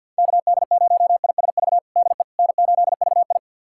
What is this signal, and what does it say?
Morse code, heard as a keyed tone, about 46 wpm, 700 Hz. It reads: KD0ISV DE N8VI
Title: USA Callsign DE Callsign 50wpm